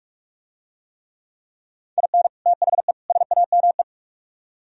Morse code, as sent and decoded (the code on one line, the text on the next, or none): .. -. / - .... / ... .- -- .
IN THE SAME